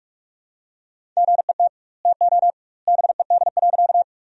Morse code, heard as a keyed tone, about 46 wpm, 700 Hz. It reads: GET TO BED.
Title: 2x Word-Spacing - Sentences from Top 500 Words - 45wpm